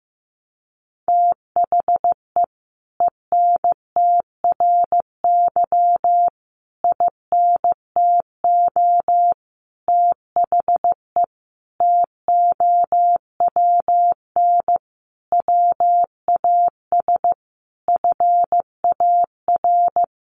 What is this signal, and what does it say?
Morse code, heard as a keyed tone, about 15 wpm, 700 Hz. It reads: THE ENTRY INTO THE TOWN WAS FAR
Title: Sentences from Top 1500 Words 15wpm